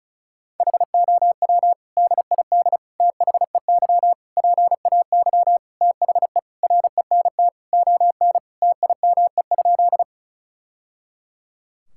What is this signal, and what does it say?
Morse code, heard as a keyed tone, about 34 wpm, 700 Hz. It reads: HOW DID THEY PAY THE RENT ON TIME?
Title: Sentences from Top 1000 Words - Review Part 5 - 35wpm